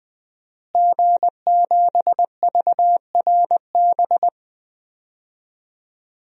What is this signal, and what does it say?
Morse code, heard as a keyed tone, about 20 wpm, 700 Hz. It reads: G7VRB
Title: DXCC-100 Call Signs - 5 Characters - Repeated 3x - 20wpm